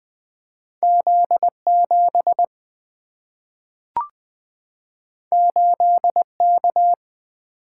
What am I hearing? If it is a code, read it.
Morse code, heard as a keyed tone, about 20 wpm, 700 Hz. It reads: Z7 8K
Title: Single-Letter Single-Number 20wpm